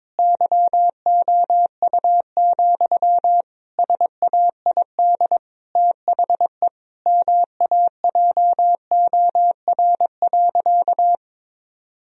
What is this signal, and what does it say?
Morse code, heard as a keyed tone, about 22 wpm, 700 Hz. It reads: YOU, SAID THE MAJOR.
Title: Sentences from Top 1000 Words - Review Part 4 - 22wpm